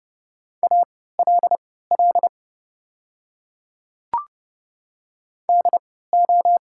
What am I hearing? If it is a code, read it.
Morse code, heard as a keyed tone, about 32 wpm, spaced as for 17 wpm, 690 Hz. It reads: ALL DO
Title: 3x Character Spacing - Top 100 Words - 30wpm